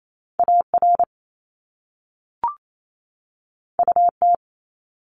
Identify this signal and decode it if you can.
Morse code, heard as a keyed tone, about 28 wpm, 700 Hz. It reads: AR UT
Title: ARRL Sections 28wpm